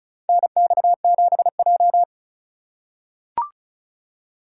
Morse code, heard as a keyed tone, about 36 wpm, 700 Hz. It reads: NX7J
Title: USA Call Signs - 4 Characters - 3x Character Spacing - 35wpm